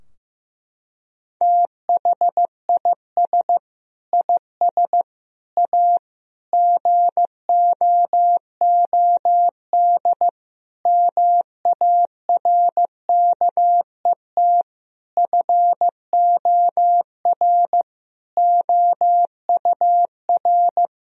Morse code, heard as a keyed tone, about 15 wpm, 700 Hz. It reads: THIS IS A GOOD MARKET FOR OUR
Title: Sentences from Top 1000 Words - Review Part 3 - 15wpm